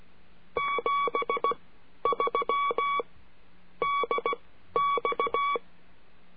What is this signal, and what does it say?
Morse code, heard as a keyed tone, about 16 wpm, spaced as for 11 wpm, 1130 Hz. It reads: 73 DX
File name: CURSO DE TELEGRAFIA